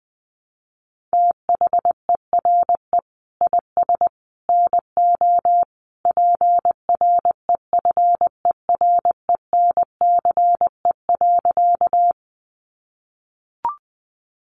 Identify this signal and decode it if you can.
Morse code, heard as a keyed tone, about 20 wpm, 700 Hz. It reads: THERE IS NO PREFERENCE.
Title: Sentences from Top 1500 Words 20wpm